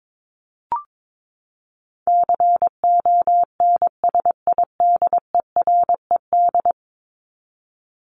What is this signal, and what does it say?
Morse code, heard as a keyed tone, about 22 wpm, 700 Hz. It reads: CONSIDERED
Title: ED Suffix - Top 100 Words with Base Words 22wpm